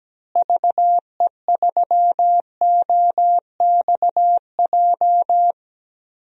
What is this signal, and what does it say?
Morse code, heard as a keyed tone, about 17 wpm, 700 Hz. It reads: VE3OXJ
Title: DXCC-100 Call Signs - 6 Characters - Repeated 3x - 17wpm